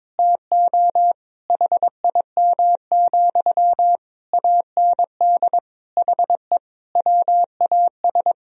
Morse code, heard as a keyed tone, about 22 wpm, 700 Hz. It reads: TO HIM, AND HE WAS
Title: Sentences from Top 1000 Words - Review Part 4 - 22wpm